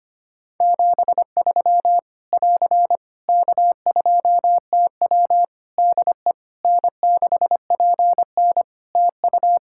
Morse code, heard as a keyed tone, about 25 wpm, 700 Hz. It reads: 73 <AR> K2TW DE N6PN TU
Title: Practice QSO - Exchange 4 - Ending - 25wpm